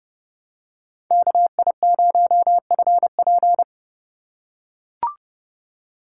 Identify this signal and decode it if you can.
Morse code, heard as a keyed tone, about 30 wpm, 700 Hz. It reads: KI0FP
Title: USA Call Signs - 5 Characters - 2x Character Spacing - 30wpm